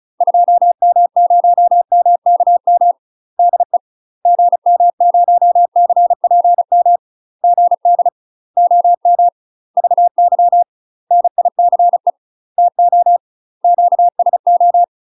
Morse code, heard as a keyed tone, about 34 wpm, 700 Hz. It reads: 2M0MKM DE GM0CPM GD OM VY NICE TO QSO
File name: UK MASTER QSOs TRAINING / 35WPM / UK MASTER QSOs 0031